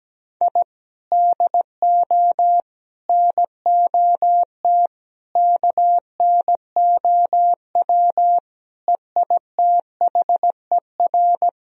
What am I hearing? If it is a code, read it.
Morse code, heard as a keyed tone, about 17 wpm, 700 Hz. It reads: I DO NOT KNOW EITHER
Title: Sentences from Top 1000 Words - Review Part 4 - 17wpm